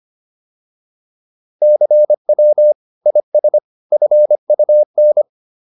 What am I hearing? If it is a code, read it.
Morse code, heard as a keyed tone, about 25 wpm, 600 Hz. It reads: CW IS FUN